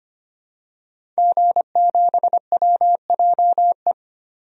Morse code, heard as a keyed tone, about 25 wpm, 700 Hz. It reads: G7WJE
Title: DXCC-100 Call Signs - 5 Characters - 25wpm